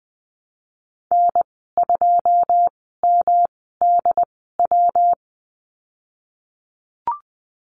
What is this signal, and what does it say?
Morse code, heard as a keyed tone, about 20 wpm, spaced as for 15 wpm, 700 Hz. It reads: N2MDW
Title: 154 Lesson 25g - Letter p - Call Signs 20wpm